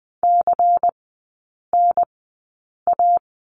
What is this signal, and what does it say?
Morse code, heard as a keyed tone, about 20 wpm, spaced as for 8 wpm, 700 Hz. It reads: CNA
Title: 138 Lesson 23e - Letter f - Three Characters 20wpm